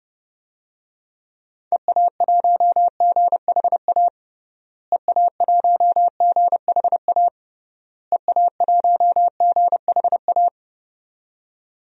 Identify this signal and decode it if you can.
Morse code, heard as a keyed tone, about 30 wpm, 700 Hz. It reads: EA1GHA EA1GHA EA1GHA
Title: DXCC-100 Call Signs - 6 Characters - Repeated 3x - 30wpm